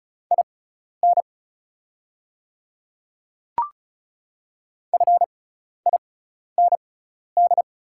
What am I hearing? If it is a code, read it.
Morse code, heard as a keyed tone, about 36 wpm, spaced as for 12 wpm, 700 Hz. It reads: IN FIND
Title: 6x Character Spacing - Top 100 Words - 35wpm